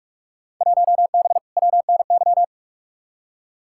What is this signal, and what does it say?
Morse code, heard as a keyed tone, about 44 wpm, 700 Hz. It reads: 1B WNY
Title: Field Day Exchanges - 45wpm